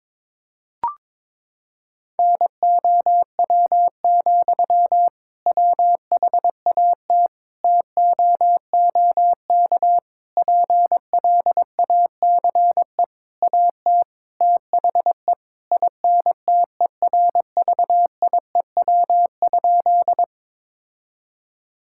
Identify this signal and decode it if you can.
Morse code, heard as a keyed tone, about 22 wpm, 700 Hz. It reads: NOW, WHAT TOOK PLACE AT THE INTERVIEW?
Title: Sentences from Top 1600 Words 22wpm